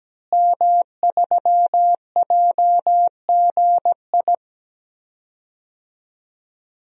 Morse code, heard as a keyed tone, about 17 wpm, 700 Hz. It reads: M3JGI
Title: DXCC-100 Call Signs - 5 Characters - Repeated 2x - 17wpm